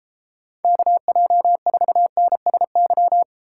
Morse code, heard as a keyed tone, about 32 wpm, 700 Hz. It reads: KJ4NSY